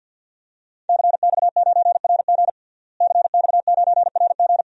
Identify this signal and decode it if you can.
Morse code, heard as a keyed tone, about 52 wpm, 700 Hz. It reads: CX9RG CX9RG
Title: DXCC-100 Call Signs - 5 Characters - Repeated 2x - 50wpm